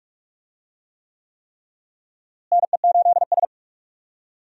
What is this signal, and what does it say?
Morse code, heard as a keyed tone, about 42 wpm, 700 Hz. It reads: NE8S
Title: USA Call Signs - 4 Characters - Repeated 3x - 45wpm